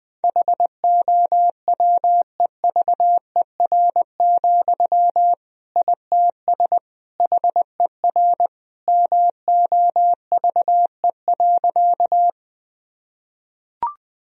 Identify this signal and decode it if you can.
Morse code, heard as a keyed tone, about 21 wpm, 700 Hz. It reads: HOWEVER, ITS HER MOVE.
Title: Sentences from Top 1000 Words - Review Part 1 - 20wpm